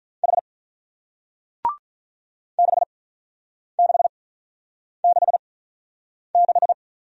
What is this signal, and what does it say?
Morse code, heard as a keyed tone, about 46 wpm, 700 Hz. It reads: H 6 6 6 6